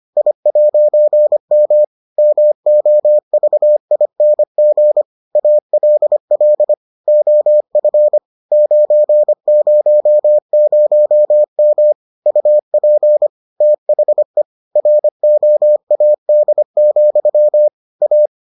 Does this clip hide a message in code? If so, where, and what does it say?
Yes: Morse code from 0.2 to 18.3 s, heard as a keyed tone, about 25 wpm, 600 Hz. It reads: I'M MOVING ALL OF 900M UP THE ROAD, A